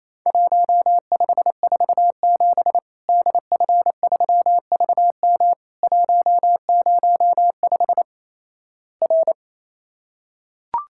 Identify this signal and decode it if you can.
Morse code, heard as a keyed tone, about 28 wpm, 700 Hz. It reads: 1547 DF3VM 105 R
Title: Worked All Europe DX Contest - QTC Exchange - 28wpm